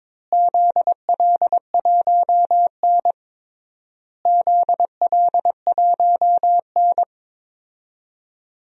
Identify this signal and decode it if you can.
Morse code, heard as a keyed tone, about 22 wpm, 700 Hz. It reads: ZL1N ZL1N
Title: DXCC-100 Call Signs - 4 Characters - Repeated 3x - 22wpm